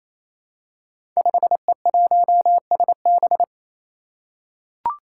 Morse code, heard as a keyed tone, about 27 wpm, 700 Hz. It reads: 5E1SB
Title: Five Challenging Letter-Number Combinations 28wpm